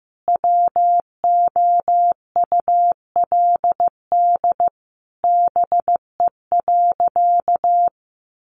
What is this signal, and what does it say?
Morse code, heard as a keyed tone, about 15 wpm, 700 Hz. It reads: WOULD BE.
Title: Sentences from Top 200 Words 15wpm